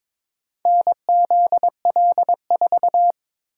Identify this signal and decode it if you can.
Morse code, heard as a keyed tone, about 22 wpm, 700 Hz. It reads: NZL4
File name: practice / 22wpm / Four Letter-Number Combinations - 22wpm